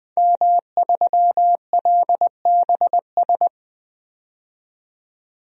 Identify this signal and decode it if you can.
Morse code, heard as a keyed tone, about 20 wpm, 700 Hz. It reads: M3LBS
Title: DXCC-100 Call Signs - 5 Characters - 20wpm